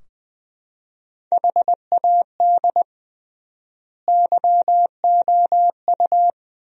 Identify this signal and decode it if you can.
Morse code, heard as a keyed tone, about 20 wpm, 700 Hz. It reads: HAD YOU
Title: 3x Word-Spacing - Sentences from Top 400 Words - 20wpm